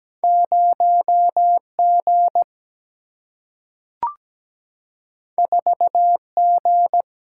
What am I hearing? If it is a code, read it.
Morse code, heard as a keyed tone, about 17 wpm, 700 Hz. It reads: 0 G 4 G
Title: Single-Letter Single-Number 17wpm